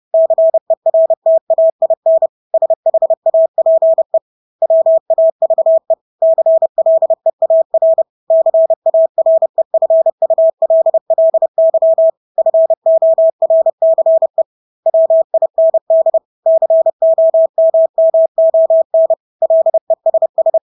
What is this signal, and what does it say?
Morse code, heard as a keyed tone, about 30 wpm, 650 Hz. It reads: CERTAIN SHAPE WAVE CLEAR CAREFULLY FORCE WIND COMMON LESS
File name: Engelske ord 30-30 wpm